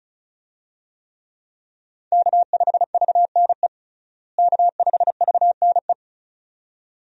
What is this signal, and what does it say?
Morse code, heard as a keyed tone, about 36 wpm, 700 Hz. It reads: K5VNE K5VNE
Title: USA Call Signs - 5 Characters - Repeated 2x - 35wpm